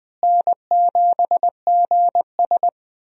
Morse code, heard as a keyed tone, about 20 wpm, 700 Hz. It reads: N7GS
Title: USA Call Signs - 4 Characters - Repeated 2x - 20wpm